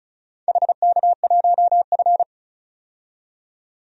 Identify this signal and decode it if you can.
Morse code, heard as a keyed tone, about 36 wpm, 700 Hz. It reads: HK1F
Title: DXCC-100 Call Signs - 4 Characters - 35wpm